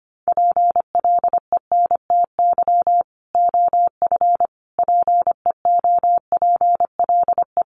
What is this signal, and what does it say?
Morse code, heard as a keyed tone, about 25 wpm, 700 Hz. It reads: PLENTY OF PEOPLE
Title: Sentences from Top 1300 Words 25wpm